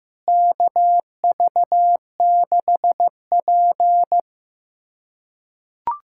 Morse code, heard as a keyed tone, about 15 wpm, 700 Hz. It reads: KV6P
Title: USA Callsign DE Callsign - 4 Characters - 15wpm